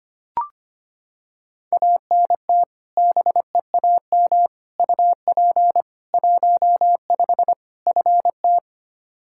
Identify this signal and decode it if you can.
Morse code, heard as a keyed tone, about 25 wpm, 700 Hz. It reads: ANT BEAM UP 15 FT
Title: Practice QSO - Exchange 2 - Info - 25wpm